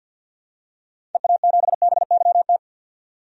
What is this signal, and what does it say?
Morse code, heard as a keyed tone, about 46 wpm, 700 Hz. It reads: EA7BYT
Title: DXCC-100 Call Signs - 6 Characters - 50wpm